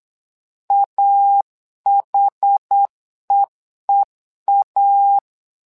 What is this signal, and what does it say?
Morse code, heard as a keyed tone, about 8 wpm, 800 Hz. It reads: AHEEA
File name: LEZIONI / 2 UHDN / 2 UH 8wpm 2129